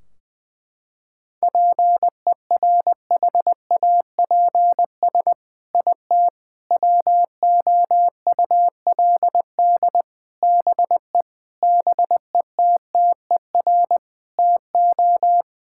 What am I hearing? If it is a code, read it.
Morse code, heard as a keyed tone, about 20 wpm, 700 Hz. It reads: PERHAPS IT WOULD BE BETTER TO
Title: Sentences from Top 1600 Words 20wpm